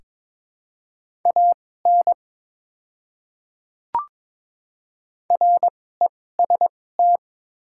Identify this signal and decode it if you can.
Morse code, heard as a keyed tone, about 22 wpm, spaced as for 16 wpm, 700 Hz. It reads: AN REST
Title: 2x Character Spacing - Top 500 Words - 22wpm